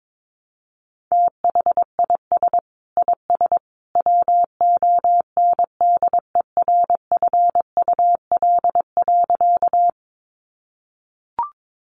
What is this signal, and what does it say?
Morse code, heard as a keyed tone, about 22 wpm, 700 Hz. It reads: THIS IS WONDERFUL.
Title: Sentences from Top 1100 Words 22wpm